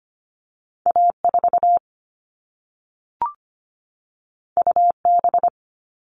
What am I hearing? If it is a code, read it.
Morse code, heard as a keyed tone, about 25 wpm, 700 Hz. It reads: A4 UB
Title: All Cut Numbers - 2-Digit 25wpm